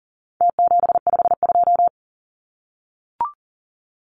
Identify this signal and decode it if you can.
Morse code, heard as a keyed tone, about 39 wpm, 700 Hz. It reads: T752
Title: Numbers - Contest Style - 4-Digit 40wpm